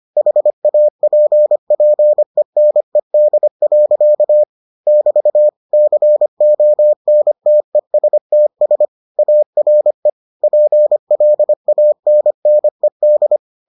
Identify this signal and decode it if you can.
Morse code, heard as a keyed tone, about 25 wpm, 600 Hz. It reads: HAPPENED. <BT> CONTESTS ARE PLANNED